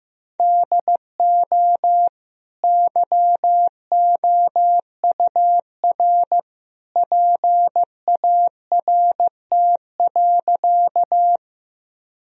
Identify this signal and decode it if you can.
Morse code, heard as a keyed tone, about 14 wpm, 700 Hz. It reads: DO YOUR PART.
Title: Sentences from Top 1000 Words - Review Part 1 - 15wpm